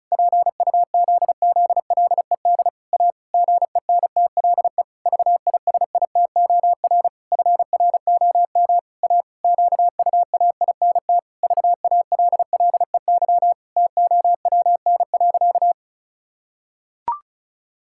Morse code, heard as a keyed tone, about 36 wpm, 700 Hz. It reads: PUZZLED A GENTLE VISITOR FROM A QUAINT VALLEY TOWN.
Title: Pangram Challenge v2 - 35wpm